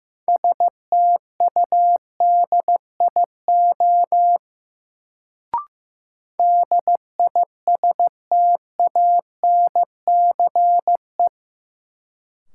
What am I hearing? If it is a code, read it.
Morse code, heard as a keyed tone, about 16 wpm, 700 Hz. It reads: STUDIO DISTANCE
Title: Sentences from Top 1200 Words - New Word Drill - 15wpm